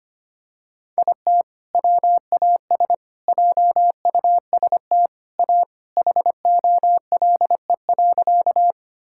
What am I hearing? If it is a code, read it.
Morse code, heard as a keyed tone, about 25 wpm, 700 Hz. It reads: IT WAS JUST A HOLE.
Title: Sentences from Top 1600 Words 25wpm